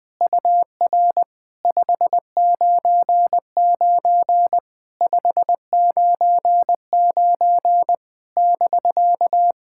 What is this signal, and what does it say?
Morse code, heard as a keyed tone, about 21 wpm, 700 Hz. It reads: UR 599 599 <BK>
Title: SOTA QSO Practice 20wpm